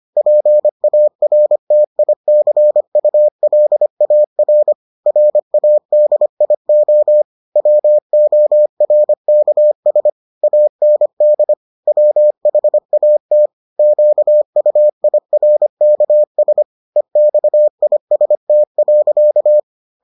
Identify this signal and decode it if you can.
Morse code, heard as a keyed tone, about 25 wpm, 600 Hz. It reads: PARTICULAR RADIO WORKS AND WHAT QUIRKS EXIST.